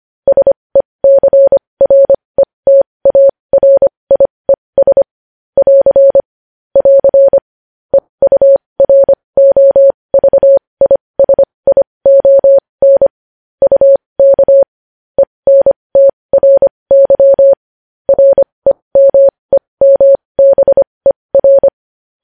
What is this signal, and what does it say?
Morse code, heard as a keyed tone, about 25 wpm, spaced as for 20 wpm, 550 Hz. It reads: SECRETARIES <AR> <AR> EUROVISION UK ENTRY REMEMBER
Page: news-headlines-in-morse-code-at-20-wpm